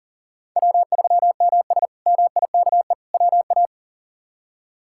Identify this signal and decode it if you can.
Morse code, heard as a keyed tone, about 40 wpm, 700 Hz. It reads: W3MS MIKE WA
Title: K1USN SST QSO Practice - 40wpm